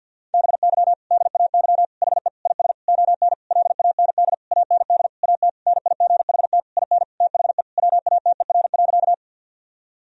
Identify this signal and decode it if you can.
Morse code, heard as a keyed tone, about 53 wpm, 700 Hz. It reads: BY DAY HE IS ON LAND AND AT NIGHT IN THE WATER.
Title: Sentences from Top 1000 Words - Review Part 1 - 50wpm